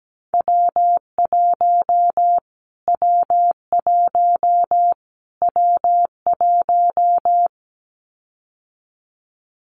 Morse code, heard as a keyed tone, about 17 wpm, 700 Hz. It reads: W1 W1 W1
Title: Callsign Prefixes for Top 25 DXCC - Repeated 3x - 17wpm